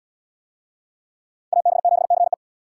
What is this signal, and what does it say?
Morse code, heard as a keyed tone, about 72 wpm, 700 Hz. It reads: AB7CE